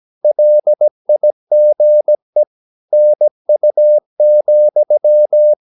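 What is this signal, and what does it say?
Morse code, heard as a keyed tone, about 17 wpm, 600 Hz. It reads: LIGE NU,
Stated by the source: Slow (QRS) Morse practice at 17 wpm, from CW med Gnister.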